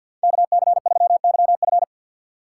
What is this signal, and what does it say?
Morse code, heard as a keyed tone, about 53 wpm, 690 Hz. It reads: KC3YF
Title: USA Call Signs - 5 Characters - 6x Character Spacing - 50wpm